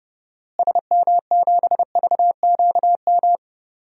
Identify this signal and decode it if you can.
Morse code, heard as a keyed tone, about 32 wpm, 700 Hz. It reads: SM7VQM
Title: DXCC-100 Call Signs - 6 Characters - Repeated 2x - 30wpm